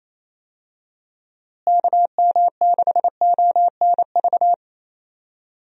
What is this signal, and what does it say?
Morse code, heard as a keyed tone, about 28 wpm, 700 Hz. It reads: KM6ONV
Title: USA Call Signs - 6 Characters - Repeated 3x - 28wpm